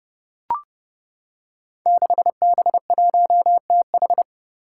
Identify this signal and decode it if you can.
Morse code, heard as a keyed tone, about 30 wpm, 700 Hz. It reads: 6B1TH
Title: Five Challenging Letter-Number Combinations 30wpm